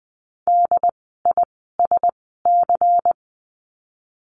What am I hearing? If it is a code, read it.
Morse code, heard as a keyed tone, about 20 wpm, spaced as for 14 wpm, 700 Hz. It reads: DISC
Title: 094 Lesson 16f - Letter c - Words 20wpm